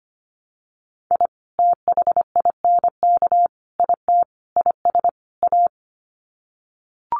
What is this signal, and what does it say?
Morse code, heard as a keyed tone, about 25 wpm, 700 Hz. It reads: I THINK IT IS A
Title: Sets of 5 Words Encore 25wpm